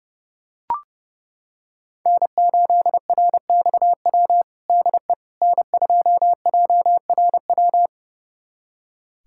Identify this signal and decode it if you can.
Morse code, heard as a keyed tone, about 30 wpm, 700 Hz. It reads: N8RXW DE N2JRW
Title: USA Callsign DE Callsign - 5 Characters - 30wpm